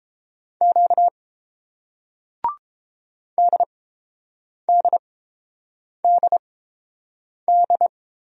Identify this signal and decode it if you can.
Morse code, heard as a keyed tone, about 30 wpm, spaced as for 7 wpm, 700 Hz. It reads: Q DDDD